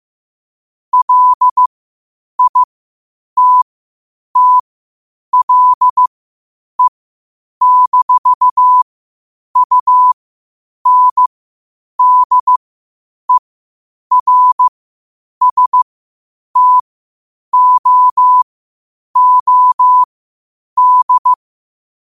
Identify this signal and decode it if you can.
Morse code, heard as a keyed tone, about 15 wpm, spaced as for 8 wpm, 1000 Hz. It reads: LITTLE-UNDERSTOOD
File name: qotdfiles / 8wpm